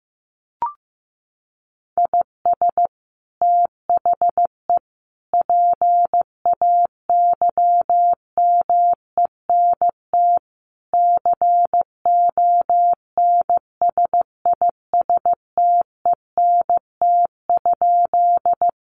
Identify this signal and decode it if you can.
Morse code, heard as a keyed tone, about 15 wpm, 700 Hz. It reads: IS THE PAYMENT CONSISTENT?
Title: Sentences from Top 1500 Words 15wpm